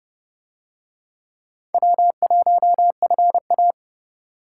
Morse code, heard as a keyed tone, about 26 wpm, 700 Hz. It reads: W1FA